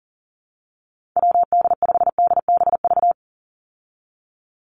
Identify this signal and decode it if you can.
Morse code, heard as a keyed tone, about 39 wpm, 700 Hz. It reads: WD5DBV
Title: Call Signs Repeated 2x - 40wpm